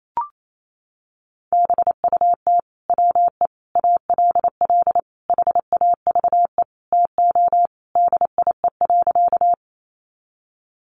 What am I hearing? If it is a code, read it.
Morse code, heard as a keyed tone, about 28 wpm, 700 Hz. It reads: BUT WE ALL HAVE TO DIE.
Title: Sentences from Top 1300 Words 28wpm